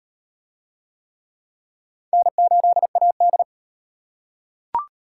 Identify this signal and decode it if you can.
Morse code, heard as a keyed tone, about 34 wpm, 700 Hz. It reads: N8AD